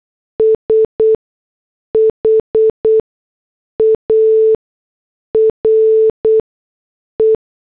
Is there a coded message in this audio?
Yes: Morse code from 0.4 to 7.4 s, heard as a keyed tone, about 8 wpm, spaced as for 6 wpm, 440 Hz. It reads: SHARE